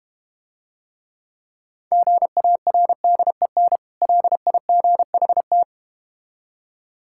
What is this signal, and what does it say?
Morse code, heard as a keyed tone, about 31 wpm, 700 Hz. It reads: GARDEN LIGHT